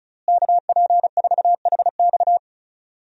Morse code, heard as a keyed tone, about 36 wpm, 700 Hz. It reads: KP4HX
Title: USA Call Signs - 5 Characters - Repeated 3x - 35wpm